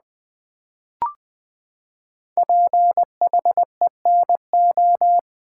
Morse code, heard as a keyed tone, about 20 wpm, 700 Hz. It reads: PHENO
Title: Build A Word - 5 Letters At A Time - 10-Letter Words - 20wpm